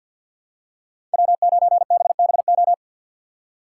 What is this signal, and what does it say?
Morse code, heard as a keyed tone, about 52 wpm, 700 Hz. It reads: W9BBO
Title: USA Call Signs - 5 Characters - 3x Character Spacing - 50wpm